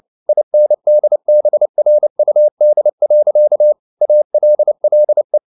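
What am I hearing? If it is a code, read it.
Morse code, heard as a keyed tone, about 29 wpm, 600 Hz. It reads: INDBRUD. ALLE
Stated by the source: Fast (QRQ) Morse at 29 wpm, from CW med Gnister.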